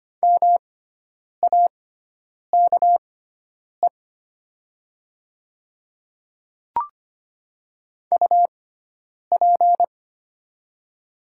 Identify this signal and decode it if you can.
Morse code, heard as a keyed tone, about 25 wpm, spaced as for 9 wpm, 700 Hz. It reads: MAKE UP
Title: 6x Character Spacing - Top 500 Words - 25wpm